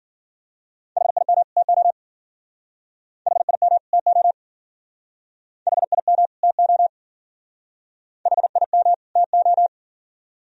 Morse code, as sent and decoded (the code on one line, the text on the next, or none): .... .. -- / - --- / .... .. -- / - --- / .... .. -- / - --- / .... .. -- / - ---
HIM TO HIM TO HIM TO HIM TO